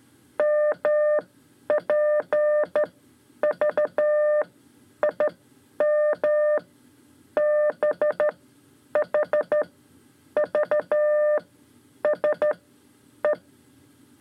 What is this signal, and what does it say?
Morse code, heard as a keyed tone, about 13 wpm, spaced as for 10 wpm, 600 Hz. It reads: MP V IM B H V S E
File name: 5 Lezione CW 2